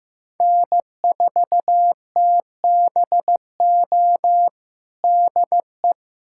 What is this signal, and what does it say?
Morse code, heard as a keyed tone, about 16 wpm, 700 Hz. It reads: N4TBO DE
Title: USA Callsign DE Callsign - 5 Characters - 15wpm